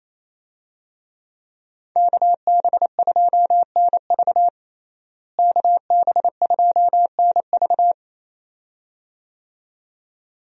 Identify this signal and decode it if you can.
Morse code, heard as a keyed tone, about 27 wpm, 700 Hz. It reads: KB2NV KB2NV
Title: USA Call Signs - 5 Characters - Repeated 2x - 28wpm